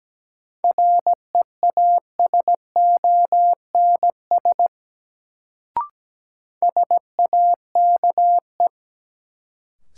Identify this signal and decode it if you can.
Morse code, heard as a keyed tone, about 17 wpm, 700 Hz. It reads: REASONS SAKE
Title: Sets of 4 Words - New Word Drill - 17wpm